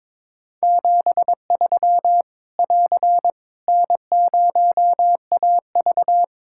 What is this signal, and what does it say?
Morse code, heard as a keyed tone, about 22 wpm, 700 Hz. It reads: 73 <AR> N0AV
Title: Practice QSO - Exchange 4 - Ending - 22wpm